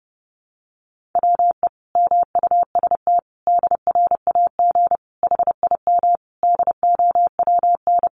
Morse code, heard as a keyed tone, about 30 wpm, 700 Hz. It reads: WE MUST DRAG HIM DOWN
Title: Sentences from Top 1500 Words 30wpm